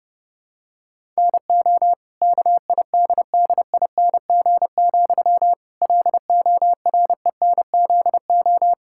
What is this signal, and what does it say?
Morse code, heard as a keyed tone, about 30 wpm, 700 Hz. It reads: NO KIDDING, LORENZO
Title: Pangram Challenge v2 - 30wpm